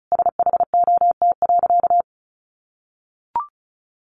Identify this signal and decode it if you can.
Morse code, heard as a keyed tone, about 34 wpm, 700 Hz. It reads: SHOT.
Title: Sentences from Top 1100 Words 35wpm